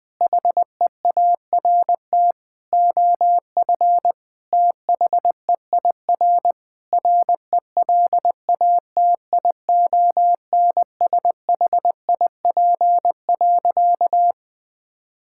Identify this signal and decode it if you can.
Morse code, heard as a keyed tone, about 20 wpm, 700 Hz. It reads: HEART OF THEIR RELATIONSHIP.
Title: Sentences with 5-Word Phrases - 20wpm